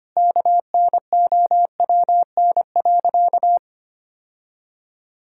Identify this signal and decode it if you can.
Morse code, heard as a keyed tone, about 25 wpm, 700 Hz. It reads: KNOWN.
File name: practice / 25wpm / Sentences with 3-Word Phrases - 25wpm